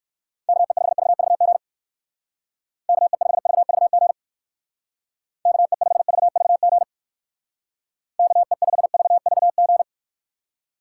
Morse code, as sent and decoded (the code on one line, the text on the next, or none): -.- . ..... ...- ...- --. / -.- . ..... ...- ...- --. / -.- . ..... ...- ...- --. / -.- . ..... ...- ...- --.
KE5VVG KE5VVG KE5VVG KE5VVG